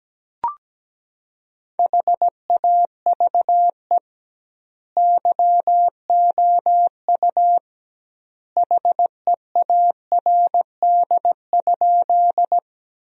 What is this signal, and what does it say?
Morse code, heard as a keyed tone, about 17 wpm, 700 Hz. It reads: HAVE YOU HEARD?
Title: 2x Word-Spacing - Sentences from Top 400 Words - 17wpm